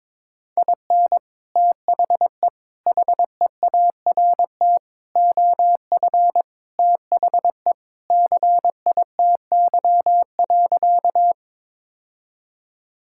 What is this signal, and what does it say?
Morse code, heard as a keyed tone, about 23 wpm, 700 Hz. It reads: IN THE HEART OF THE CITY.
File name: practice / 22wpm / Sentences with 3-Word Phrases - 22wpm